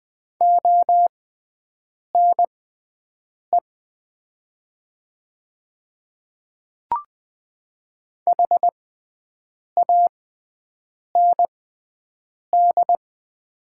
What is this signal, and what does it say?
Morse code, heard as a keyed tone, about 20 wpm, spaced as for 7 wpm, 700 Hz. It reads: ONE HAND
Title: 6x Character Spacing - Top 200 Words - 20wpm